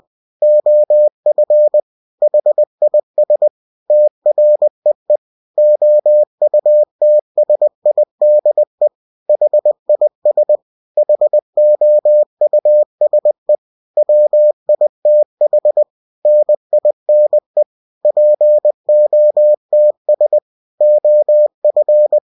Morse code, heard as a keyed tone, about 20 wpm, 600 Hz. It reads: OF HIS TREE OUTSIDE HIS HOUSE WITH NINE POTS OF